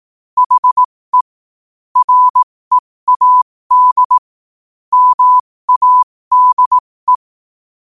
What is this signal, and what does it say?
Morse code, heard as a keyed tone, about 18 wpm, spaced as for 15 wpm, 1000 Hz. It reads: HE READ MADE
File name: qotdfiles / 15wpm